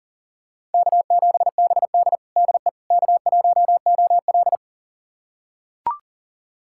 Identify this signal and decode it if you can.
Morse code, heard as a keyed tone, about 37 wpm, 700 Hz. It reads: K7BD DE K1OL
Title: USA Callsign DE Callsign - 4 Characters - 40wpm